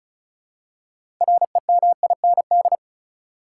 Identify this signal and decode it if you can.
Morse code, heard as a keyed tone, about 36 wpm, 700 Hz. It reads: REMIND
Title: Six Letter Words - 35wpm